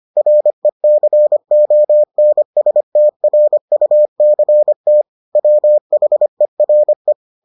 Morse code, heard as a keyed tone, about 25 wpm, 600 Hz. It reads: RECONSTRUCT WHERE